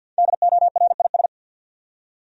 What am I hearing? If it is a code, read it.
Morse code, heard as a keyed tone, about 53 wpm, 700 Hz. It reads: DORIS
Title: USA Common Names - 50wpm